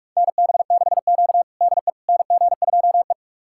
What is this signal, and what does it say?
Morse code, heard as a keyed tone, about 44 wpm, 690 Hz. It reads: NB6Q DE NG2E
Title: USA Callsign DE Callsign - 4 Characters - 45wpm